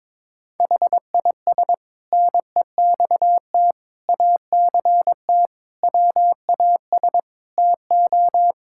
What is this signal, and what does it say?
Morse code, heard as a keyed tone, about 22 wpm, 700 Hz. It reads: HIS NEXT ACT WAS TO
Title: Sentences from Top 1000 Words - Review Part 5 - 22wpm